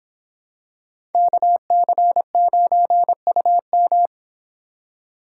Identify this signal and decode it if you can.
Morse code, heard as a keyed tone, about 26 wpm, 700 Hz. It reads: KC9UM